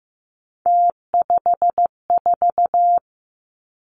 Morse code, heard as a keyed tone, about 15 wpm, 700 Hz. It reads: T54
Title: Three Challenging Letter-Number Combinations 15wpm